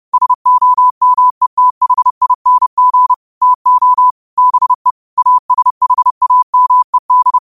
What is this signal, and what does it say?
Morse code, heard as a keyed tone, about 30 wpm, 1000 Hz. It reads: SOMETHING TO BE ASHAMED